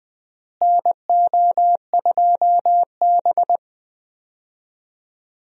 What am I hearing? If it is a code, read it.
Morse code, heard as a keyed tone, about 20 wpm, 700 Hz. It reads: NO2B
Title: USA Call Signs - 4 Characters - Repeated 3x - 20wpm